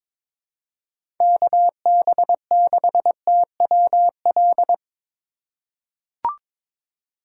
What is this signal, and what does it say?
Morse code, heard as a keyed tone, about 21 wpm, 700 Hz. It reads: KB6TWL
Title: USA Call Signs - 6 Characters - 2x Character Spacing - 22wpm